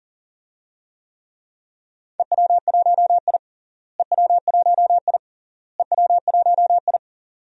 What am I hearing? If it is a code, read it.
Morse code, heard as a keyed tone, about 40 wpm, 700 Hz. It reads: EW1I EW1I EW1I
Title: DXCC-100 Call Signs - 4 Characters - Repeated 3x - 40wpm